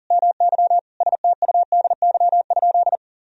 Morse code, heard as a keyed tone, about 40 wpm, 700 Hz. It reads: MY STUDY?
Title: Sentences from Top 1000 Words - Review Part 1 - 40wpm